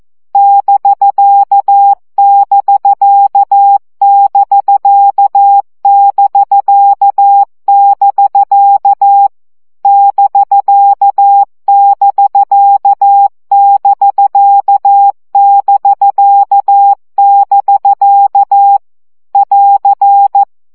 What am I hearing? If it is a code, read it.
Morse code, heard as a keyed tone, about 14 wpm, 800 Hz. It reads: <BK><BK><BK><BK><BK> <BK><BK><BK><BK><BK> <AR>
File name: CW gyakorlas bk 12 baud